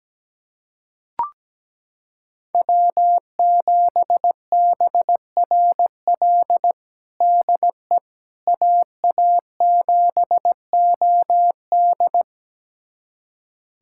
Morse code, heard as a keyed tone, about 17 wpm, 700 Hz. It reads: W7BRL DE AA7OD
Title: USA Callsign DE Callsign - 5 Characters - 17wpm